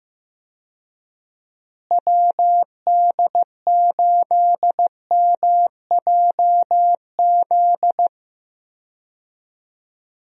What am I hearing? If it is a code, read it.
Morse code, heard as a keyed tone, about 15 wpm, 700 Hz. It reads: WD8MJZ
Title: USA Call Signs - 6 Characters - Repeated 3x - 15wpm